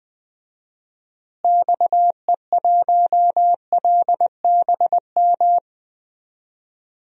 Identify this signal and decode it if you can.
Morse code, heard as a keyed tone, about 20 wpm, 700 Hz. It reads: XE1LBM
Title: DXCC-100 Call Signs - 6 Characters - Repeated 2x - 20wpm